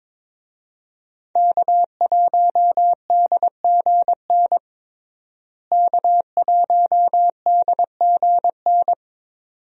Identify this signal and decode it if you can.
Morse code, heard as a keyed tone, about 22 wpm, 700 Hz. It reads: K1DGN K1DGN
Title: USA Call Signs - 5 Characters - Repeated 2x - 22wpm